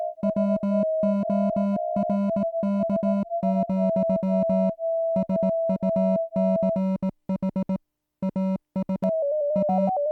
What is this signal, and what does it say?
Morse code, heard as a keyed tone, about 18 wpm, 200 Hz. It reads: WORK, SUCH AS A